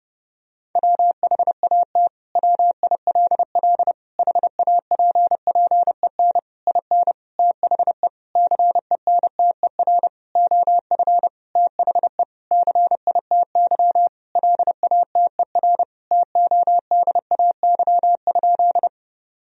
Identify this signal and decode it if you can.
Morse code, heard as a keyed tone, about 32 wpm, 700 Hz. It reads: WHAT WILL HAPPEN IN THE CENTER OF THE CITY LATER TODAY?
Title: Sentences from Top 1000 Words - Review Part 2 - 30wpm